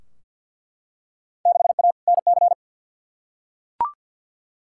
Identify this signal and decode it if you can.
Morse code, heard as a keyed tone, about 48 wpm, 700 Hz. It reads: 6A NC
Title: Field Day Exchanges Encore - 50wpm